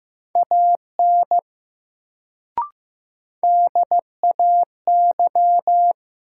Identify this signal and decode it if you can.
Morse code, heard as a keyed tone, about 15 wpm, 700 Hz. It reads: AN DAY
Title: Sets of 2 Words - New Word Drill - 15wpm